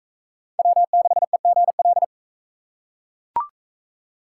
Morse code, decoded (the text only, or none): W6EGL